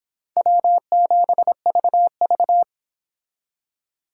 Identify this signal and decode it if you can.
Morse code, heard as a keyed tone, about 26 wpm, 700 Hz. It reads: W7VV